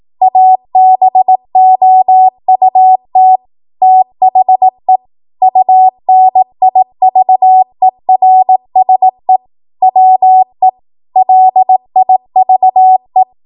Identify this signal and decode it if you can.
Morse code, heard as a keyed tone, about 18 wpm, 750 Hz. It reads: ABOUT THE UNIVERSE WE LIVE
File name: Morse / 18WPM